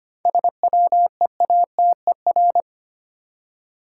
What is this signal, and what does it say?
Morse code, heard as a keyed tone, about 25 wpm, 700 Hz. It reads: SWEATER
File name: practice / 25wpm / Easy Words with Hints - 25wpm